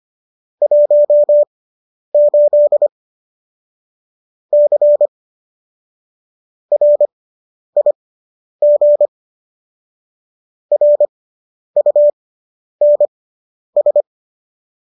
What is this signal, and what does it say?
Morse code, heard as a keyed tone, about 25 wpm, spaced as for 10 wpm, 590 Hz. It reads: 18 C RIG RUNS